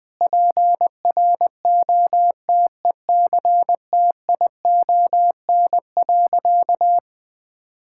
Morse code, heard as a keyed tone, about 20 wpm, 700 Hz. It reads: PROTECTION.
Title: Sentences from Top 1600 Words 20wpm